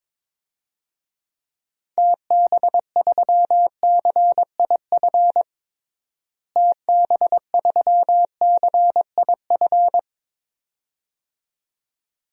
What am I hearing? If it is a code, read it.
Morse code, heard as a keyed tone, about 22 wpm, 700 Hz. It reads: TB3CIF TB3CIF
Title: DXCC-100 Call Signs - 6 Characters - Repeated 2x - 22wpm